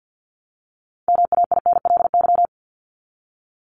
Morse code, heard as a keyed tone, about 50 wpm, 700 Hz. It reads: MAINLY
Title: LY Suffix - Top 100 Words with Base Words 50wpm